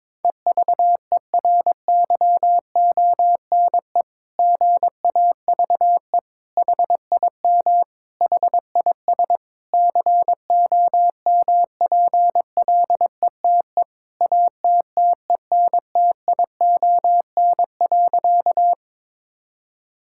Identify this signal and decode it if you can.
Morse code, heard as a keyed tone, about 22 wpm, 700 Hz. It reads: EVERYONE GAVE HIM HIS COMPLETE ATTENTION.
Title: Sentences from Top 1000 Words - Review Part 6 - 22wpm